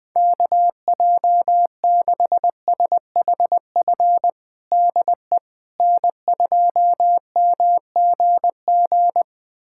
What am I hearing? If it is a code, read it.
Morse code, heard as a keyed tone, about 20 wpm, 700 Hz. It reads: KJ6SHF DE N2MGG
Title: USA Callsign DE Callsign 20wpm